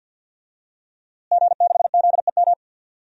Morse code, heard as a keyed tone, about 50 wpm, 700 Hz. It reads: G6ZEM
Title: DXCC-100 Call Signs - 5 Characters - Repeated 3x - 50wpm